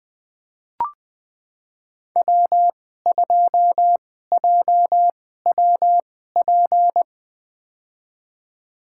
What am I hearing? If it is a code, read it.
Morse code, heard as a keyed tone, about 21 wpm, spaced as for 15 wpm, 700 Hz. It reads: W2JWP
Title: USA Call Signs - 5 Characters - 2x Character Spacing - 20wpm